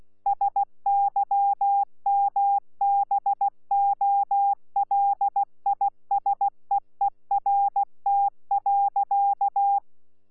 Morse code, heard as a keyed tone, about 16 wpm, 800 Hz. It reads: SYMBOLISEERT.
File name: Z PI4VRZ / PI4VRZ 14-jun-2025